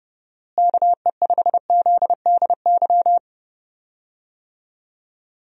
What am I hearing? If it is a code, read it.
Morse code, heard as a keyed tone, about 32 wpm, 700 Hz. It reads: KE5ZDY
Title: USA Call Signs - 6 Characters - Repeated 3x - 30wpm